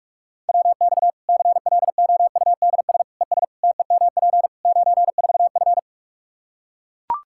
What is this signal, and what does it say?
Morse code, heard as a keyed tone, about 41 wpm, 700 Hz. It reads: WX CLOUDS ES TEMP 94F
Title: Practice QSO - Exchange 2 - Info - 45wpm